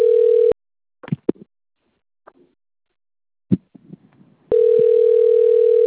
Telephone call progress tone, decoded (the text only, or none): ringback tone